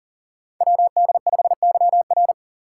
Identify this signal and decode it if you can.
Morse code, heard as a keyed tone, about 42 wpm, 700 Hz. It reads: WD5YR
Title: Code Groups - 5 Letter:Numbers - 40wpm